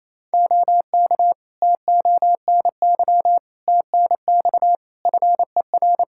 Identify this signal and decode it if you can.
Morse code, heard as a keyed tone, about 28 wpm, 700 Hz. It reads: OK TONY TNX FER
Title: Practice QSO - Exchange 4 - Ending - 28wpm